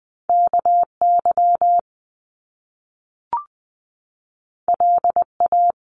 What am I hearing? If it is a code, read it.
Morse code, heard as a keyed tone, about 20 wpm, 700 Hz. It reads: KY LA
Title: US States 20wpm